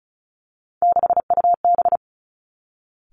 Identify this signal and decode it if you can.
Morse code, heard as a keyed tone, about 36 wpm, 700 Hz. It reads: 6UB
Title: Three Challenging Letter-Number Combinations 35wpm